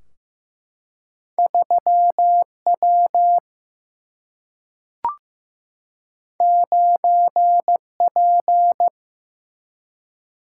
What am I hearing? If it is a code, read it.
Morse code, heard as a keyed tone, about 16 wpm, 700 Hz. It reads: 3W 9P
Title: Single-Letter Single-Number 15wpm